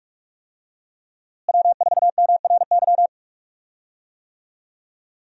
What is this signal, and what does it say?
Morse code, heard as a keyed tone, about 41 wpm, 700 Hz. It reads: W4MRY
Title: USA Call Signs - 5 Characters - Repeated 2x - 45wpm